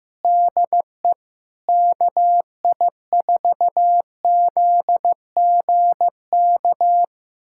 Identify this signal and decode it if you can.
Morse code, heard as a keyed tone, about 16 wpm, 700 Hz. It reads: DE KI4ZGK
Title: USA Callsign DE Callsign - 6 Characters - 15wpm